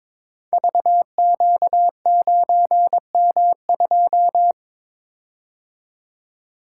Morse code, heard as a keyed tone, about 22 wpm, 700 Hz. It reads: VQ9M2
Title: Code Groups - 5 Letter:Numbers - 22wpm